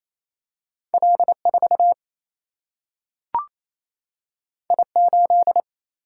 Morse code, heard as a keyed tone, about 28 wpm, 700 Hz. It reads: L4 I8
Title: Single-Letter Single-Number 28wpm